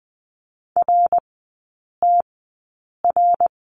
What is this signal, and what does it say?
Morse code, heard as a keyed tone, about 20 wpm, spaced as for 8 wpm, 700 Hz. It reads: RTR
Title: 053 Lesson 10e - Letter r - Three Characters 20wpm